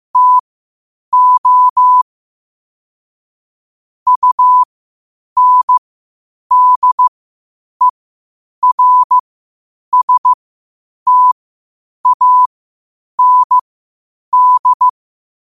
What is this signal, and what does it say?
Morse code, heard as a keyed tone, about 15 wpm, spaced as for 8 wpm, 1000 Hz. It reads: TO UNDERSTAND
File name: qotdfiles / 8wpm